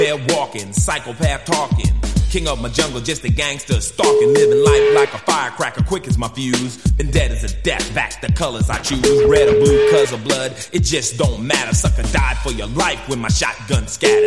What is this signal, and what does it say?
Ringback tone (Europe, 425 Hz), with speech.
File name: itc8TXYlOGurPeI